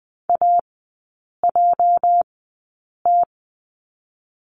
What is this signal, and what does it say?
Morse code, heard as a keyed tone, about 20 wpm, spaced as for 8 wpm, 700 Hz. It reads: AJT
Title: 205 Lesson 33e - Letter j - Three Characters 20wpm